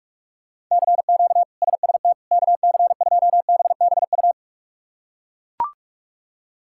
Morse code, heard as a keyed tone, about 41 wpm, 700 Hz. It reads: CQ SST KC2BBU
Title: K1USN SST QSO Practice - 45wpm